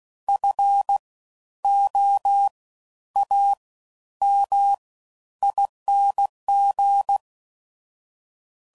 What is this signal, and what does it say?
Morse code, heard as a keyed tone, about 16 wpm, spaced as for 10 wpm, 800 Hz. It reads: FOAMING
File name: Suffixes / Special INGs / Fast Ings